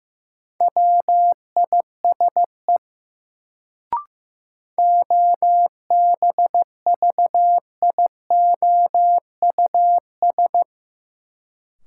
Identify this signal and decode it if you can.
Morse code, heard as a keyed tone, about 15 wpm, 700 Hz. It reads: WISE OBVIOUS
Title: Sentences from Top 1600 Words - New Word Drill - 15wpm